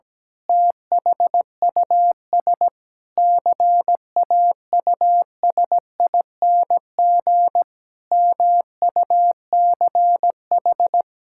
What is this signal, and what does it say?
Morse code, heard as a keyed tone, about 17 wpm, 700 Hz. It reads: THUS CAUSING MUCH
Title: Pangram Challenge v2 - 17wpm